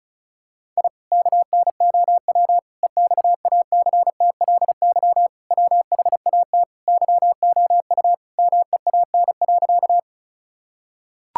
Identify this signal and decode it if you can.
Morse code, heard as a keyed tone, about 36 wpm, 700 Hz. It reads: I KNOW EXACTLY WHAT YOU MEAN.
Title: Sentences from Top 1000 Words - Review Part 6 - 35wpm